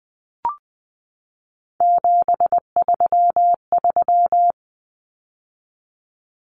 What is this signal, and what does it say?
Morse code, heard as a keyed tone, about 20 wpm, 700 Hz. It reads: 733
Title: Common Cut Numbers - 3-Digit 20wpm